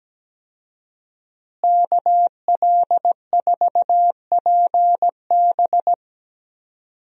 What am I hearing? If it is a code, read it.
Morse code, heard as a keyed tone, about 17 wpm, 700 Hz. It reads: KL4PB
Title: USA Call Signs - 5 Characters - Repeated 3x - 17wpm